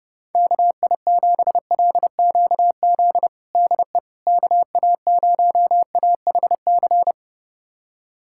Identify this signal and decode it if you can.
Morse code, heard as a keyed tone, about 30 wpm, 700 Hz. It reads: KI7LQZ DE KA0AHC
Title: USA Callsign DE Callsign 30wpm